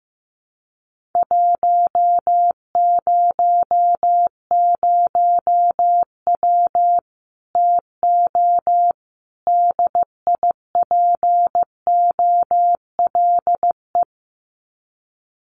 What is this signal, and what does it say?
Morse code, heard as a keyed tone, about 15 wpm, 700 Hz. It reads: 100W TO DIPOLE
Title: Common QSO Phrases 15wpm